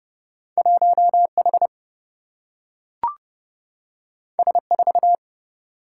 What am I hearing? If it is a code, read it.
Morse code, heard as a keyed tone, about 30 wpm, 700 Hz. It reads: 1H S4
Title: Single-Letter Single-Number 30wpm